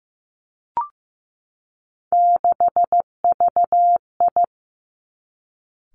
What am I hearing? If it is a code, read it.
Morse code, heard as a keyed tone, about 15 wpm, 700 Hz. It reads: 6VI
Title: Three Challenging Letter-Number Combinations 15wpm